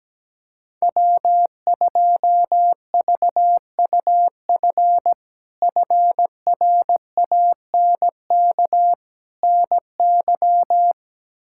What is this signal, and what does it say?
Morse code, heard as a keyed tone, about 17 wpm, 700 Hz. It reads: W2VUF FRANK NY
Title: K1USN SST QSO Practice - 17wpm